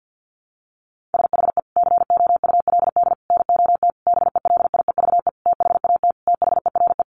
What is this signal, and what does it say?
Morse code, heard as a keyed tone, about 48 wpm, 700 Hz. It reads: SHE COULD NOT BELIEVE THAT THERE
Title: Sentences from Top 1000 Words 50wpm